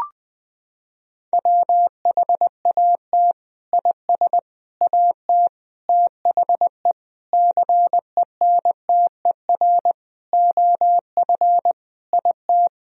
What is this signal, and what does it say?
Morse code, heard as a keyed tone, about 20 wpm, 700 Hz. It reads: WHAT IS AT THE CENTER OF IT
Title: Sentences from Top 1000 Words - Review Part 2 - 20wpm